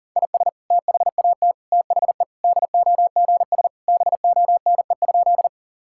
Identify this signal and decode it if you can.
Morse code, heard as a keyed tone, about 40 wpm, 700 Hz. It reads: IS THAT THE DOGS BONE?
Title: Sentences from Top 1600 Words 40wpm